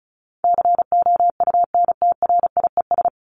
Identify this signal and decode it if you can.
Morse code, heard as a keyed tone, about 34 wpm, 700 Hz. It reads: COUNTRIES
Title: ES:S Suffix - Top 100 Words with Base Words 35wpm